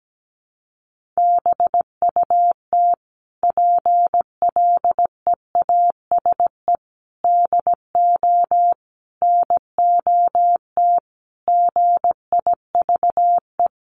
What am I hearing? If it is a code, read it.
Morse code, heard as a keyed tone, about 17 wpm, 700 Hz. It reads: BUT PLEASE DO NOT GIVE
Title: Sentences from Top 1500 Words 17wpm